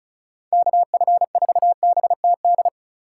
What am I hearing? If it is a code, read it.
Morse code, heard as a keyed tone, about 34 wpm, 700 Hz. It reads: KF4BTD
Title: USA Call Signs - 6 Characters - 4x Character Spacing - 35wpm